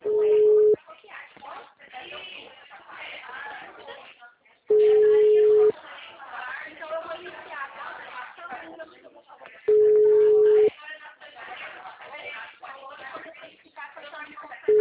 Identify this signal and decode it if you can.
Ringback tone (Europe, 425 Hz), with speech.
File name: out- -2023- - - .18802